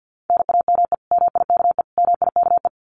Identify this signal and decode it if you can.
Morse code, heard as a keyed tone, about 50 wpm, 700 Hz. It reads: NAME MIKE MIKE
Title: Common QSO Phrases Encore 50wpm